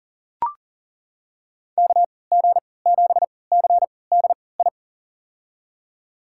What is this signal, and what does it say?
Morse code, heard as a keyed tone, about 38 wpm, spaced as for 22 wpm, 700 Hz. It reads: KG7CDI
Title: USA Call Signs - 6 Characters - 3x Character Spacing - 40wpm